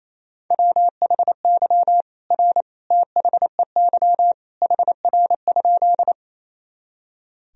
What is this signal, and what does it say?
Morse code, heard as a keyed tone, about 27 wpm, 700 Hz. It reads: WHY R THEY HR?
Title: Sentences from Top 200 Words and Abbreviations 28wpm